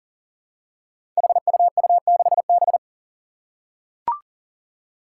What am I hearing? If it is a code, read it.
Morse code, heard as a keyed tone, about 39 wpm, 700 Hz. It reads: HUU6B
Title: Code Groups - 5 Letter:Numbers - 40wpm